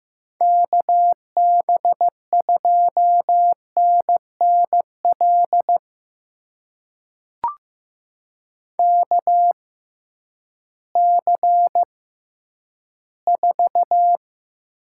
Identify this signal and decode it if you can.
Morse code, heard as a keyed tone, about 15 wpm, spaced as for 10 wpm, 700 Hz. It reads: KB2NNL KC4
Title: USA Call Signs - 6 Characters - 6x Character Spacing - 15wpm